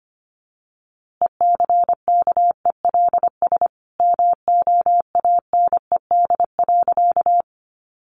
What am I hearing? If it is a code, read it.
Morse code, heard as a keyed tone, about 24 wpm, 700 Hz. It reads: ECKELS MOANED.
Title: A Sound Of Thunder 25wpm